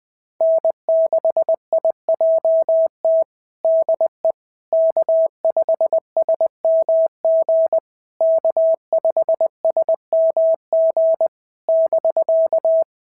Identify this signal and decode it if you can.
Morse code, heard as a keyed tone, about 20 wpm, 690 Hz. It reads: N6IJT DE K5SMG K5SMG <BK>
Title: SOTA QSO Practice 20wpm